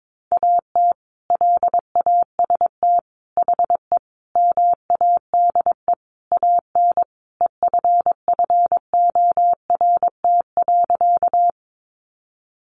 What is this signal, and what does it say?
Morse code, heard as a keyed tone, about 22 wpm, 700 Hz. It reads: AT LAST HE MADE AN EFFORT.
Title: Sentences from Top 1200 Words 22wpm